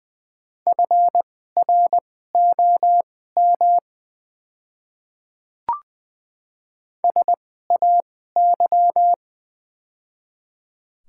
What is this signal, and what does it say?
Morse code, heard as a keyed tone, about 20 wpm, spaced as for 14 wpm, 690 Hz. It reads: FROM SAY
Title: 2x Character Spacing - Top 300 Words - 20wpm